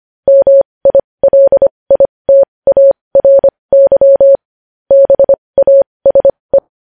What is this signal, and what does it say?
Morse code, heard as a keyed tone, about 25 wpm, spaced as for 20 wpm, 550 Hz. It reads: MILITARY BASE
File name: podcast / News headlines-1x-25x20